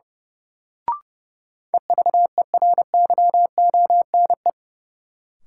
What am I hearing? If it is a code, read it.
Morse code, heard as a keyed tone, about 30 wpm, 700 Hz. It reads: EVERYONE
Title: Sentences from Top 800 Words - New Word Drill - 30wpm